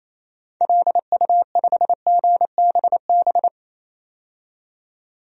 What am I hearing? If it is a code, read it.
Morse code, heard as a keyed tone, about 28 wpm, 700 Hz. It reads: LU5GBB
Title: DXCC-100 Call Signs - 6 Characters - 28wpm